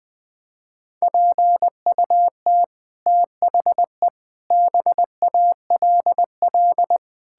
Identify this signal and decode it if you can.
Morse code, heard as a keyed tone, about 20 wpm, 700 Hz. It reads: PUT THE BALL
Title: Sets of 9 Words - 20wpm